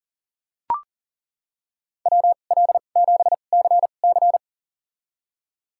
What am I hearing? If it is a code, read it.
Morse code, heard as a keyed tone, about 40 wpm, spaced as for 29 wpm, 700 Hz. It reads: WL7CC
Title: USA Call Signs - 5 Characters - 2x Character Spacing - 40wpm